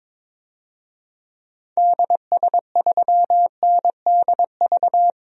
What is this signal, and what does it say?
Morse code, heard as a keyed tone, about 22 wpm, 700 Hz. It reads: DS3NDV
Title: DXCC-100 Call Signs - 6 Characters - Repeated 2x - 22wpm